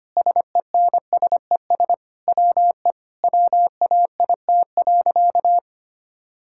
Morse code, heard as a keyed tone, about 25 wpm, 700 Hz. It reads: SENSES WE WAIT.
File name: practice / 25wpm / Sentences with 9-Word Phrases - 25wpm